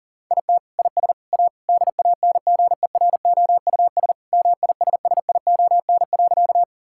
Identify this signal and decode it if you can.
Morse code, heard as a keyed tone, about 37 wpm, 700 Hz. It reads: IT IS A DANGEROUS MISSION.
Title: Sentences from Top 1600 Words 40wpm